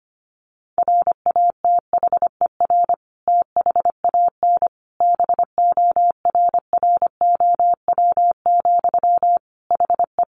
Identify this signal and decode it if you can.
Morse code, heard as a keyed tone, about 25 wpm, 700 Hz. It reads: RATHER THAN BORROW, HE
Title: Sentences from Top 1500 Words 25wpm